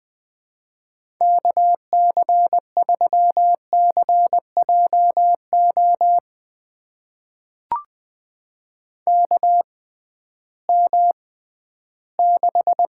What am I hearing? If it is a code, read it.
Morse code, heard as a keyed tone, about 20 wpm, spaced as for 13 wpm, 700 Hz. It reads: KC3CJO KM6
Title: USA Call Signs - 6 Characters - 6x Character Spacing - 20wpm